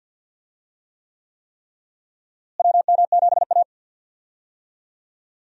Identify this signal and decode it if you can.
Morse code, heard as a keyed tone, about 47 wpm, 700 Hz. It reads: WM7A
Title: USA Call Signs - 4 Characters - Repeated 3x - 50wpm